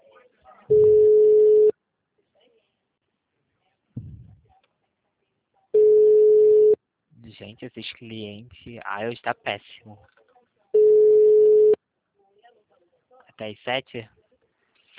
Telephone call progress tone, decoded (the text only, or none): ringback tone